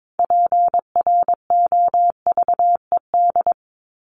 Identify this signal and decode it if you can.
Morse code, heard as a keyed tone, about 22 wpm, 700 Hz. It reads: PROVED
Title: ED Suffix - Top 100 Words with Base Words 22wpm